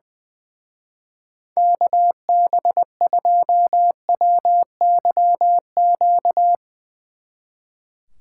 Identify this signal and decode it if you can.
Morse code, heard as a keyed tone, about 22 wpm, 700 Hz. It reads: KB2WYQ
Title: Speed Racing - USA 2x3 Call Signs 20wpm